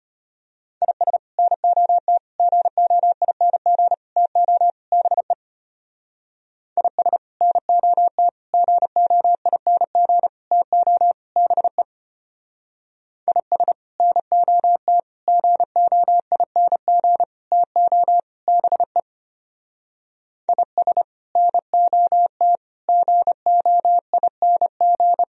Morse code, heard as a keyed tone, about 31 wpm, 700 Hz. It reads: IS NOT GOING TO BE IS NOT GOING TO BE IS NOT GOING TO BE IS NOT GOING
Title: Speed-Racing - Sets of 5 Words 25wpm